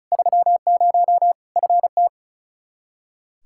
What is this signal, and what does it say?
Morse code, heard as a keyed tone, about 36 wpm, 700 Hz. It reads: 30 FT
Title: K1USN SST QSO Practice 35wpm